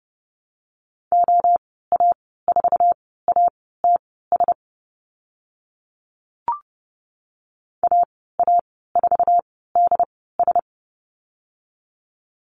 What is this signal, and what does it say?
Morse code, heard as a keyed tone, about 30 wpm, spaced as for 17 wpm, 700 Hz. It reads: OA4ATS AA4DS
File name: Lesson 14g - Number 2 - Call Signs 30wpm